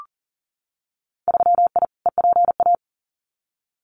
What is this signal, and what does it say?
Morse code, heard as a keyed tone, about 40 wpm, 700 Hz. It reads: 3I EPA
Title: Winter Field Day Exchanges Encore 40wpm